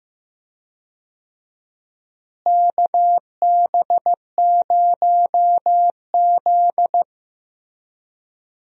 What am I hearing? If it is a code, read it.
Morse code, heard as a keyed tone, about 15 wpm, 700 Hz. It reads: KB0Z
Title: USA Call Signs - 4 Characters - Repeated 2x - 15wpm